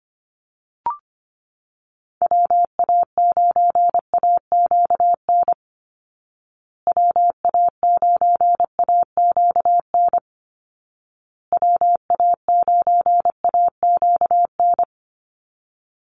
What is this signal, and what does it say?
Morse code, heard as a keyed tone, about 25 wpm, 700 Hz. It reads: WA9AQN WA9AQN WA9AQN
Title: Call Signs Repeated 2x - 25wpm